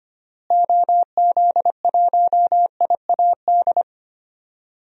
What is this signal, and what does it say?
Morse code, heard as a keyed tone, about 25 wpm, 700 Hz. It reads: OZ1IAD
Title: DXCC-100 Call Signs - 6 Characters - Repeated 2x - 25wpm